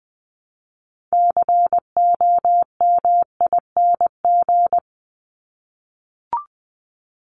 Morse code, heard as a keyed tone, about 20 wpm, 700 Hz. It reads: COMING
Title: ING Suffix - Top 100 Words with Base Words 20wpm